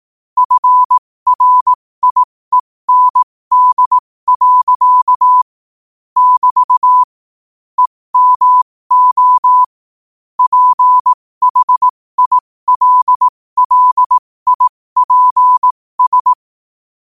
Morse code, heard as a keyed tone, about 18 wpm, spaced as for 15 wpm, 1000 Hz. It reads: FRIEND. <BT> EMO PHILLIPS
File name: qotdfiles / 15wpm